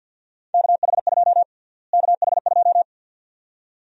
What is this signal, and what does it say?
Morse code, heard as a keyed tone, about 47 wpm, 700 Hz. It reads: KH2 KH2
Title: USA Call Sign Prefixes - Repeated 2x - 50wpm